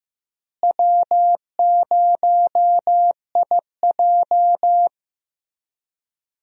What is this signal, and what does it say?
Morse code, heard as a keyed tone, about 15 wpm, 700 Hz. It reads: W0IJ
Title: USA Call Signs - 4 Characters - Repeated 2x - 15wpm